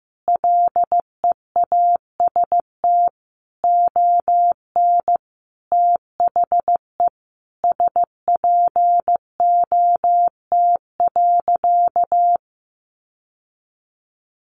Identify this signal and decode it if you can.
Morse code, heard as a keyed tone, about 15 wpm, 700 Hz. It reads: LEAST ON THE SPOT.
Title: Sentences from Top 1100 Words 15wpm